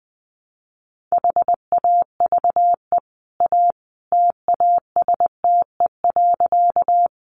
Morse code, heard as a keyed tone, about 20 wpm, 700 Hz. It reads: HAVE A TASTE.
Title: Sentences from Top 1100 Words 20wpm